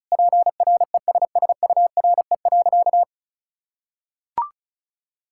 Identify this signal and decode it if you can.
Morse code, heard as a keyed tone, about 34 wpm, 700 Hz. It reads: PRESSURE.
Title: Sentences from Top 1000 Words - Review Part 6 - 35wpm